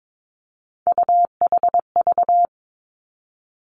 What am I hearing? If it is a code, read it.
Morse code, heard as a keyed tone, about 22 wpm, 700 Hz. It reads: UHV
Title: Three Challenging Letter-Number Combinations 22wpm